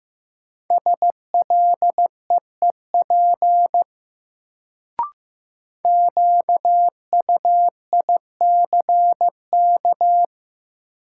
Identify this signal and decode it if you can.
Morse code, heard as a keyed tone, about 15 wpm, 700 Hz. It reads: SLEEP QUICK
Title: Sentences from Top 500 Words - New Word Drill - 15wpm